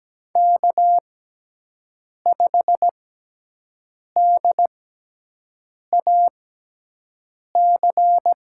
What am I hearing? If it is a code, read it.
Morse code, heard as a keyed tone, about 17 wpm, spaced as for 6 wpm, 700 Hz. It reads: K5DAC
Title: USA Call Signs - 6x Character Spacing - 17wpm